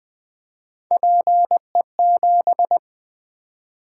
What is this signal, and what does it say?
Morse code, heard as a keyed tone, about 20 wpm, 700 Hz. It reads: PE7
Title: Three Letter-Number Combinations - 20wpm